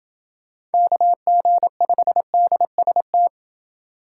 Morse code, heard as a keyed tone, about 28 wpm, 700 Hz. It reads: KG5DST